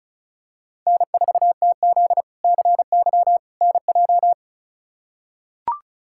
Morse code, heard as a keyed tone, about 34 wpm, 700 Hz. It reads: N4TZ CY NJ
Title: K1USN SST QSO Practice - 35wpm